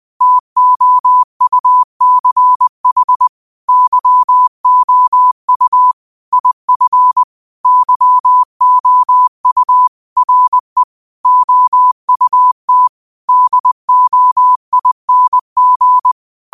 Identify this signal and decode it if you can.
Morse code, heard as a keyed tone, about 20 wpm, 1000 Hz. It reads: TOUCH YOU IF YOU RE OUT DOING
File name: qotdfiles / 20wpm